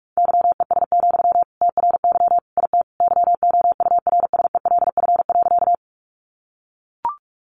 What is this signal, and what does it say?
Morse code, heard as a keyed tone, about 44 wpm, 700 Hz. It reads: YES, TRY IT YOURSELF.
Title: Sentences from Top 600 Words 45wpm